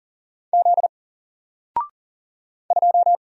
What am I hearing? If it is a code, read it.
Morse code, heard as a keyed tone, about 39 wpm, 700 Hz. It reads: Z 2
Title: Single Letter-Number Alt 40wpm